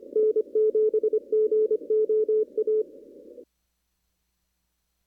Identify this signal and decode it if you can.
Morse code, heard as a keyed tone, about 25 wpm, 430 Hz. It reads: N7GOA
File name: CW / kt5j wpx cw-op-k5pi